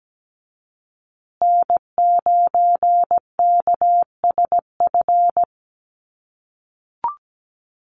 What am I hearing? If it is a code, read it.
Morse code, heard as a keyed tone, about 17 wpm, 700 Hz. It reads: N9KSF
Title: USA Call Signs Repeated 2x 17wpm